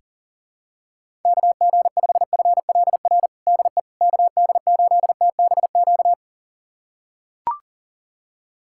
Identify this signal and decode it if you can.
Morse code, heard as a keyed tone, about 37 wpm, 700 Hz. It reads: KG5FLR DE KD8TBQ
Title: USA Callsign DE Callsign 40wpm